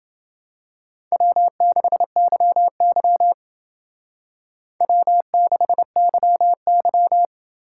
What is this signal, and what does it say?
Morse code, heard as a keyed tone, about 29 wpm, 700 Hz. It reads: W6YY W6YY